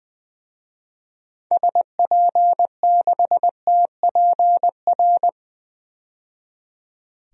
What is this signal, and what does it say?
Morse code, heard as a keyed tone, about 20 wpm, 700 Hz. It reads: SP6TPR
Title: DXCC-100 Call Signs - 6 Characters - 20wpm